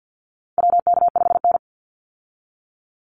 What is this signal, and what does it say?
Morse code, heard as a keyed tone, about 50 wpm, 700 Hz. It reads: WK5N
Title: Call Signs - 50wpm